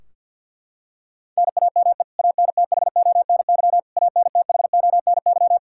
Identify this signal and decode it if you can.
Morse code, heard as a keyed tone, about 48 wpm, 700 Hz. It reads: NAME ANTHONY ANTHONY
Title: Practice QSO - Exchange 1 - Beginning - 50wpm